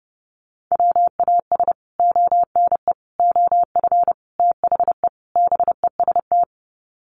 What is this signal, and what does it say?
Morse code, heard as a keyed tone, about 30 wpm, 700 Hz. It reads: WAS ONE OF THE BEST
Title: Sets of 5 Words Encore2 30wpm